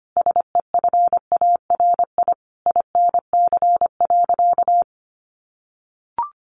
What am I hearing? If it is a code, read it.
Morse code, heard as a keyed tone, about 25 wpm, 700 Hz. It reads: SEFARI INC.
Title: A Sound Of Thunder 25wpm